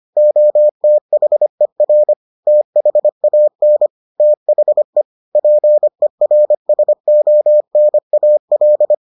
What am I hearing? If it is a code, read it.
Morse code, heard as a keyed tone, about 25 wpm, 600 Hz. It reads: OTHER THAN THE PERSONAL